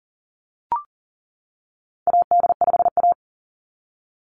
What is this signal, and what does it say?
Morse code, heard as a keyed tone, about 40 wpm, 700 Hz. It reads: AD5A
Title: NA SOTA Chaser Callsigns 40wpm